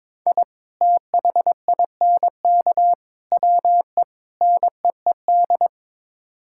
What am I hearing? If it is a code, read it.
Morse code, heard as a keyed tone, about 22 wpm, 700 Hz. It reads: I THINK WE NEED
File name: practice / 22wpm / Build A Phrase - 5-Word Phrases - 22wpm